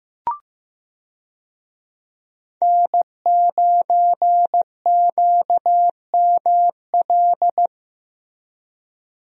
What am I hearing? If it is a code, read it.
Morse code, heard as a keyed tone, about 15 wpm, 700 Hz. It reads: N9QML
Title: USA Call Signs - 5 Characters - Repeated 2x - 15wpm